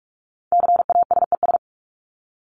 Morse code, heard as a keyed tone, about 44 wpm, 700 Hz. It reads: CASES
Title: ES:S Suffix - Top 100 Words with Base Words 45wpm